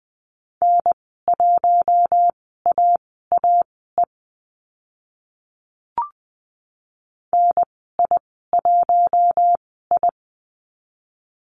Morse code, heard as a keyed tone, about 20 wpm, spaced as for 15 wpm, 700 Hz. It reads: N1AAE NI1I
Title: 042 Lesson 8g - Number 1 - Call Signs 20wpm